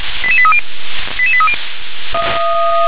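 Five-tone selective call (ZVEI), heard as quick runs of five tones, tones of about 70 ms. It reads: 88830
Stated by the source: ZVEI five-tone selective call received on SDR.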